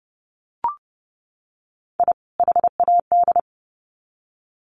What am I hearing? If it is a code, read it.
Morse code, heard as a keyed tone, about 30 wpm, 700 Hz. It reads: I HAD
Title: Sets of 2 Words 30wpm